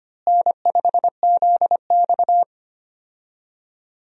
Morse code, heard as a keyed tone, about 25 wpm, 700 Hz. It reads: N5ZX
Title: USA Call Signs - 4 Characters - Repeated 3x - 25wpm